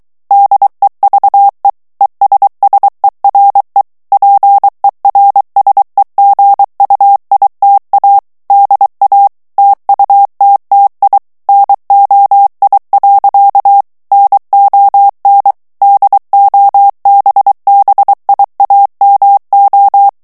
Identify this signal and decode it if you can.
Morse code, heard as a keyed tone, about 23 wpm, 800 Hz. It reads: DEVE ESSERE PERSEGUITA DA TUTTI NOI. NON DOBBIAMO
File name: morse mp3 / ITALIEN texte clair 1 vitesse 22 mots minute